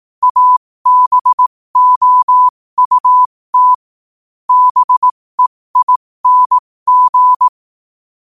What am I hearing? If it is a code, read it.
Morse code, heard as a keyed tone, about 18 wpm, spaced as for 15 wpm, 1000 Hz. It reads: ABOUT BEING
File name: qotdfiles / 15wpm